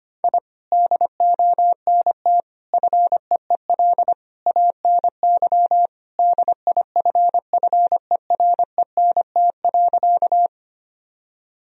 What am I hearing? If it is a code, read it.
Morse code, heard as a keyed tone, about 25 wpm, 700 Hz. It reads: I DONT FEEL ANY DIFFERENT.
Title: Sentences from Top 1000 Words - Review Part 3 - 25wpm